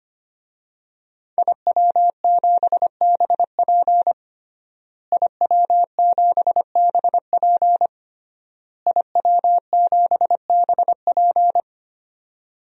Morse code, heard as a keyed tone, about 25 wpm, 700 Hz. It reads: IW7BP IW7BP IW7BP
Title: DXCC-100 Call Signs - 5 Characters - Repeated 3x - 25wpm